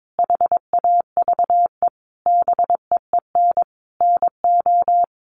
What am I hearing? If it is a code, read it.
Morse code, heard as a keyed tone, about 22 wpm, 700 Hz. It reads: HAVE BEEN NO
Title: Sentences from Top 1400 Words 22wpm